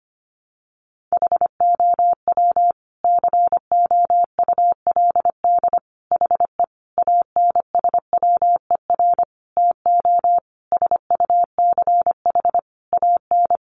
Morse code, heard as a keyed tone, about 25 wpm, 700 Hz. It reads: HOW COULD HE ANSWER TO SUCH AN
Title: Sentences from Top 1300 Words 25wpm